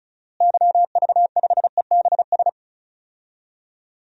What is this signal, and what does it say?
Morse code, heard as a keyed tone, about 36 wpm, 700 Hz. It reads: YV5EBS
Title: DXCC-100 Call Signs - 6 Characters - Repeated 2x - 35wpm